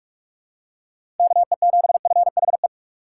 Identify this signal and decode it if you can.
Morse code, heard as a keyed tone, about 45 wpm, 700 Hz. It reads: KE7FHE
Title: USA Call Signs - 4x Character Spacing - 45wpm